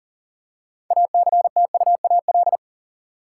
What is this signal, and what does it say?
Morse code, heard as a keyed tone, about 40 wpm, 700 Hz. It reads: ACTUAL
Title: Six Letter Words - 40wpm